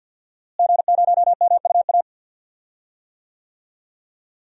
Morse code, heard as a keyed tone, about 52 wpm, 700 Hz. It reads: G0MUA
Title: DXCC-100 Call Signs - 5 Characters - Repeated 3x - 50wpm